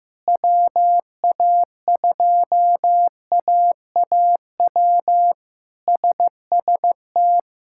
Morse code, heard as a keyed tone, about 15 wpm, 700 Hz. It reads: WA2AAW SST
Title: K1USN SST QSO Practice - 15wpm